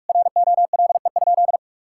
Morse code, heard as a keyed tone, about 41 wpm, 700 Hz. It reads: ROLE?
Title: Sentences from Top 1000 Words - Review Part 6 - 45wpm